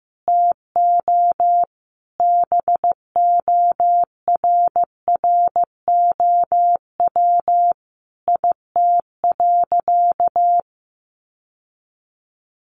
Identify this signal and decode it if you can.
Morse code, heard as a keyed tone, about 15 wpm, 700 Hz. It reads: TO BORROW IT.
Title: Sentences from Top 1500 Words 15wpm